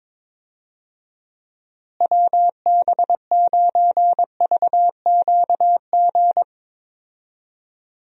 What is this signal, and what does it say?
Morse code, heard as a keyed tone, about 22 wpm, 700 Hz. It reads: WB9VQG
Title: USA Call Signs - 6 Characters - Repeated 2x - 22wpm